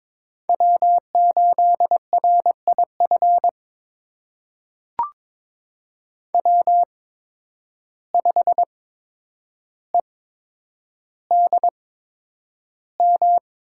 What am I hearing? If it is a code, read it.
Morse code, heard as a keyed tone, about 22 wpm, spaced as for 9 wpm, 700 Hz. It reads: W8RIF W5EDM
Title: USA Call Signs - 5 Characters - 8x Character Spacing - 22wpm